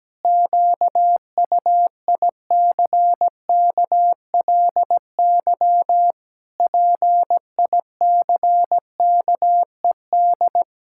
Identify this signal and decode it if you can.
Morse code, heard as a keyed tone, about 17 wpm, 700 Hz. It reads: QUICKLY PICKED
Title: Pangram Challenge v2 - 17wpm